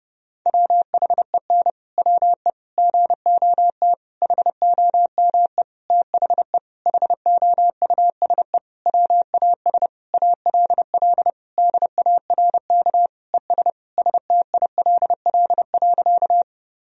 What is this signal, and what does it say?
Morse code, heard as a keyed tone, about 30 wpm, 700 Hz. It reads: WHEN WE GOT HOME THE HOUSE WAS ALL DARK ES STILL.
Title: Sentences from Top 500 Words and Abbreviations 30wpm